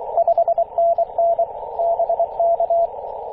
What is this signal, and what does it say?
Morse code, heard as a keyed tone, about 24 wpm, 680 Hz, with radio noise. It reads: 5NN BK
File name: posts / 2016-may-04-2203z-e6ac-21-030mhz-cw